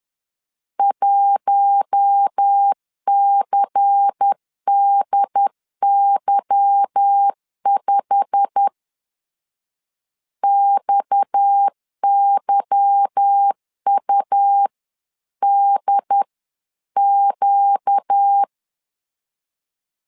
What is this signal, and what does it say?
Morse code, heard as a keyed tone, about 10 wpm, 800 Hz. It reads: 1CDY5 XYUDQ
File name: CW / DET / 1 9 1500C S B 10wpm 4439